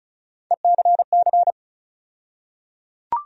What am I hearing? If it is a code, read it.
Morse code, heard as a keyed tone, about 36 wpm, 700 Hz. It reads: ECC
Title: USA Call Sign Postfixes - 3 Letters 35wpm